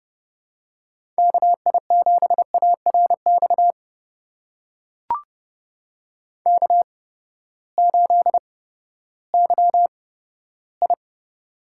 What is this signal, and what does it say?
Morse code, heard as a keyed tone, about 30 wpm, spaced as for 15 wpm, 700 Hz. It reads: KI7ARX K8YI
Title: USA Call Signs - 8x Character Spacing - 30wpm